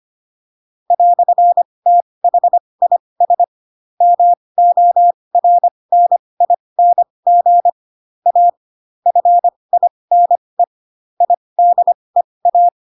Morse code, heard as a keyed tone, about 25 wpm, spaced as for 20 wpm, 700 Hz. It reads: "THIS MORNING A FINE IDEA
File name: Pinocchio / Pinocchio ch2 15 25 ik1hge